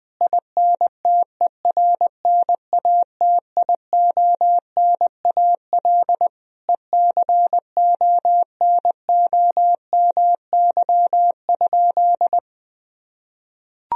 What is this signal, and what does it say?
Morse code, heard as a keyed tone, about 20 wpm, 700 Hz. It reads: INTERNATIONAL ECONOMY?
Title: Sentences from Top 1000 Words - Review Part 4 - 20wpm